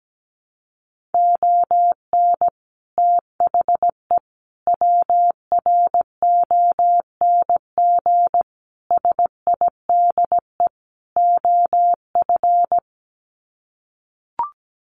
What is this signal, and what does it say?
Morse code, heard as a keyed tone, about 17 wpm, 700 Hz. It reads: ON THE WRONG SIDE OF
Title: Sets of 5 Words Encore 17wpm